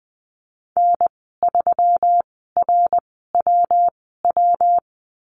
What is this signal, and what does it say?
Morse code, heard as a keyed tone, about 20 wpm, spaced as for 15 wpm, 700 Hz. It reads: N3RWW
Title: 140 Lesson 23g - Letter f - Call Signs 20wpm